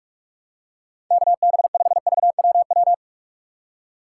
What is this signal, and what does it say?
Morse code, heard as a keyed tone, about 43 wpm, 700 Hz. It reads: K B 5 V W W
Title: USA Call Signs - 6 Characters - 6x Character Spacing - 45wpm